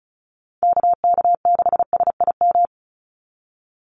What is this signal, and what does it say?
Morse code, heard as a keyed tone, about 34 wpm, 700 Hz. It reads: KK6SIM
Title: USA Call Signs 35wpm